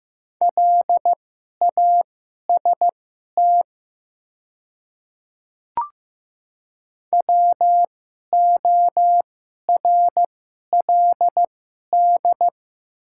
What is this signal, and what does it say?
Morse code, heard as a keyed tone, about 15 wpm, spaced as for 11 wpm, 700 Hz. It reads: LAST WORLD
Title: 2x Character Spacing - Top 300 Words - 15wpm